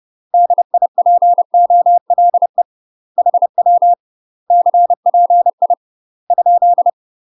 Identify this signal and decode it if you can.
Morse code, heard as a keyed tone, about 30 wpm, 700 Hz. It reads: DIPOLE HW CPI ?
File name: UK MASTER QSOs TRAINING / 30WPM / UK MASTER QSOs 0040